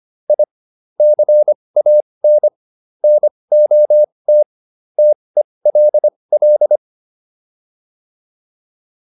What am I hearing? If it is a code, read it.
Morse code, heard as a keyed tone, about 25 wpm, spaced as for 20 wpm, 600 Hz. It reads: I CAN NOT TELL